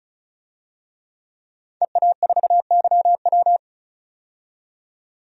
Morse code, heard as a keyed tone, about 36 wpm, 700 Hz. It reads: EA4YW
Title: DXCC-100 Call Signs - 5 Characters - 35wpm